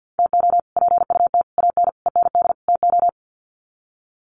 Morse code, heard as a keyed tone, about 50 wpm, 700 Hz. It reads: TO PUT AN END TO
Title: Sets of 5 Words Encore2 50wpm